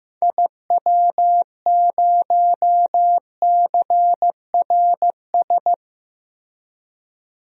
Morse code, heard as a keyed tone, about 15 wpm, 700 Hz. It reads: IW0CRS
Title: DXCC-100 Call Signs - 6 Characters - Repeated 3x - 15wpm